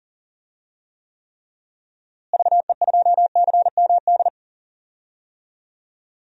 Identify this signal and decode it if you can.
Morse code, heard as a keyed tone, about 40 wpm, 700 Hz. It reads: VE2CMD
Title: DXCC-100 Call Signs - 6 Characters - Repeated 3x - 40wpm